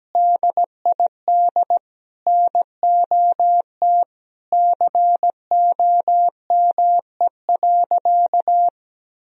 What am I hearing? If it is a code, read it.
Morse code, heard as a keyed tone, about 17 wpm, 700 Hz. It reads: DID NOT COME.
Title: Sentences from Top 1000 Words - Review Part 4 - 17wpm